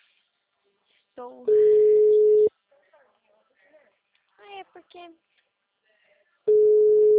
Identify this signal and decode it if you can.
Ringback tone (Europe, 425 Hz), with speech.
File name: out- -2002- - - .1084